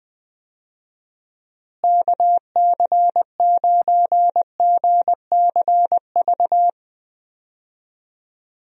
Morse code, heard as a keyed tone, about 20 wpm, 700 Hz. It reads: KC9GCV
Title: USA Call Signs - 6 Characters - Repeated 2x - 20wpm